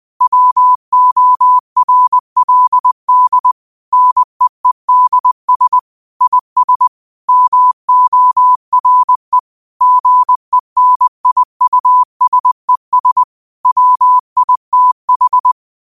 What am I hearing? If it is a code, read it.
Morse code, heard as a keyed tone, about 20 wpm, 1000 Hz. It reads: WORLD NEEDS IS MORE GENIUSES WITH